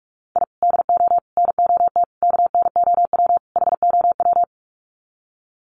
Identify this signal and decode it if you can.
Morse code, heard as a keyed tone, about 44 wpm, 700 Hz. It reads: I DO NOT KNOW HOW
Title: Sets of 5 Words 45wpm